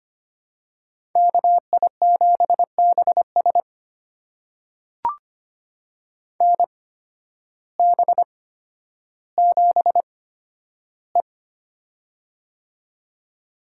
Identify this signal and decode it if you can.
Morse code, heard as a keyed tone, about 25 wpm, spaced as for 12 wpm, 700 Hz. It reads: KI7BS NB7E
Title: USA Call Signs - 8x Character Spacing - 25wpm